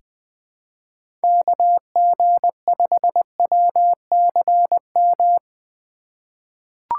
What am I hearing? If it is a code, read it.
Morse code, heard as a keyed tone, about 20 wpm, 700 Hz. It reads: KG5WCM
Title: USA Call Signs - 6 Characters - 4x Character Spacing - 20wpm